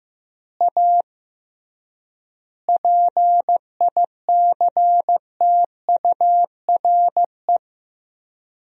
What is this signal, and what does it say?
Morse code, heard as a keyed tone, about 14 wpm, 700 Hz. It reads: A PICTURE
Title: 3x Word-Spacing - Sentences from Top 500 Words - 15wpm